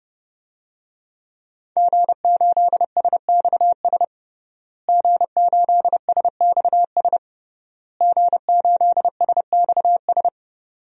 Morse code, heard as a keyed tone, about 30 wpm, 700 Hz. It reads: G8SXS G8SXS G8SXS
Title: DXCC-100 Call Signs - 5 Characters - Repeated 3x - 30wpm